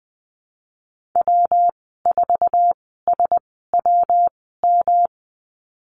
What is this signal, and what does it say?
Morse code, heard as a keyed tone, about 19 wpm, spaced as for 14 wpm, 700 Hz. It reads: W4SWM
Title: 122 Lesson 20g - Number 3 - Call Signs 20wpm